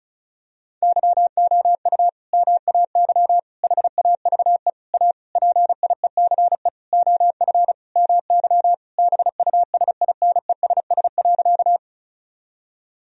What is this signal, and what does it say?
Morse code, heard as a keyed tone, about 34 wpm, 700 Hz. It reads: YOU MAY HAVE A PIECE OF MY BUSINESS.
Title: Sentences from Top 1000 Words - Review Part 2 - 35wpm